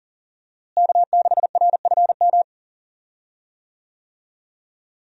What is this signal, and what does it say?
Morse code, heard as a keyed tone, about 38 wpm, 700 Hz. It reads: K6RFM
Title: USA Call Signs - 5 Characters - Repeated 2x - 40wpm